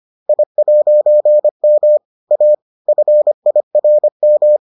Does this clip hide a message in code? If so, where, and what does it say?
Yes: Morse code from 0.3 to 4.6 s, heard as a keyed tone, about 25 wpm, 600 Hz. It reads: I'M A FIRM